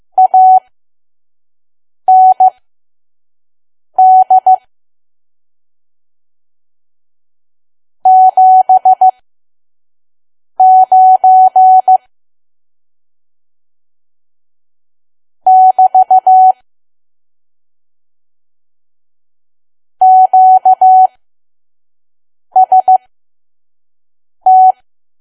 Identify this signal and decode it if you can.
Morse code, heard as a keyed tone, about 15 wpm, spaced as for 5 wpm, 750 Hz. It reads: AND 79 <BT> QST
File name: Archive / 5 WPM files / PM